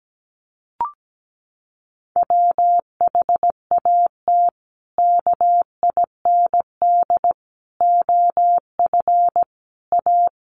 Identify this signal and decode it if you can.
Morse code, heard as a keyed tone, about 17 wpm, 700 Hz. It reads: WHAT KIND OF A
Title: Sentences from Top 400 Words 17wpm